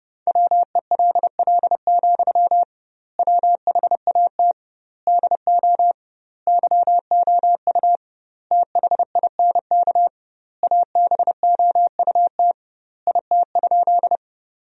Morse code, heard as a keyed tone, about 30 wpm, 700 Hz. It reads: WELL, WHAT DO YOU THINK ABOUT IT?
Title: 2x Word-Spacing - Sentences from Top 200 Words - 30wpm